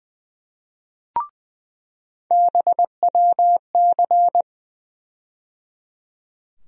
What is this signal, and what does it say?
Morse code, heard as a keyed tone, about 20 wpm, 700 Hz. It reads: BWC
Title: USA Call Sign Postfixes - 3 Letters 20wpm